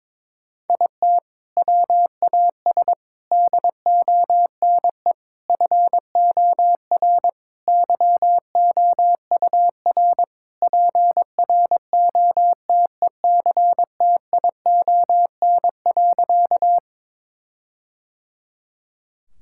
Morse code, heard as a keyed tone, about 22 wpm, 700 Hz. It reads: IT WAS DONE FOR YOUR PROTECTION.
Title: Sentences from Top 1600 Words 22wpm